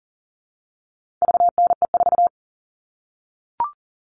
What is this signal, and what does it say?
Morse code, heard as a keyed tone, about 40 wpm, 700 Hz. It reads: VNE4
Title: All Cut Numbers - 4-Digit 40wpm